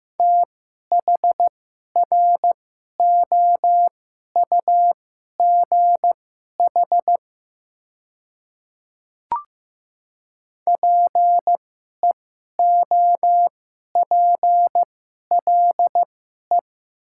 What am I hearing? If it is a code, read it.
Morse code, heard as a keyed tone, about 15 wpm, spaced as for 11 wpm, 690 Hz. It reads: THROUGH PEOPLE
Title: 2x Character Spacing - Top 200 Words - 15wpm